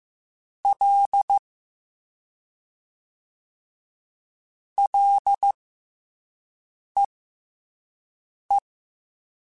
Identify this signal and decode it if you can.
Morse code, heard as a keyed tone, about 15 wpm, spaced as for 5 wpm, 800 Hz. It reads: L LEE
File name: F and L / F and L mp3 / F and L Prefixes 3x2 Twice @5